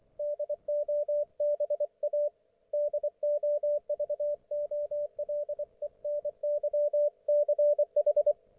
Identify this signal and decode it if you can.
Morse code, heard as a keyed tone, about 24 wpm, 590 Hz, with radio noise. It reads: DOBA DOVOLENYCH
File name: Archiv / CUC212 / ok1fhi cul212